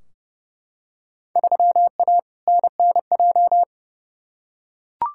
Morse code, heard as a keyed tone, about 30 wpm, 700 Hz. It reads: 3A NNJ
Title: Field Day Exchanges - 30wpm